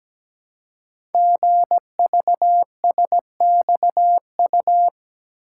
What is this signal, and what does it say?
Morse code, heard as a keyed tone, about 17 wpm, 700 Hz. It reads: GVSXU
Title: Code Groups - 5 Letters - 17wpm